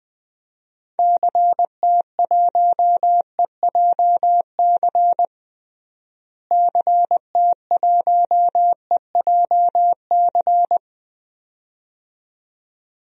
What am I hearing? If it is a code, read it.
Morse code, heard as a keyed tone, about 20 wpm, 700 Hz. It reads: CT1EJC CT1EJC
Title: DXCC-100 Call Signs - 6 Characters - Repeated 2x - 20wpm